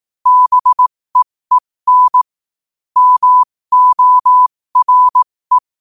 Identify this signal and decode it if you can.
Morse code, heard as a keyed tone, about 18 wpm, spaced as for 15 wpm, 1000 Hz. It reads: BEEN MORE
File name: qotdfiles / 15wpm